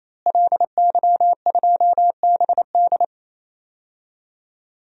Morse code, heard as a keyed tone, about 27 wpm, 700 Hz. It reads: LY2BD
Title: DXCC-100 Call Signs - 5 Characters - Repeated 3x - 28wpm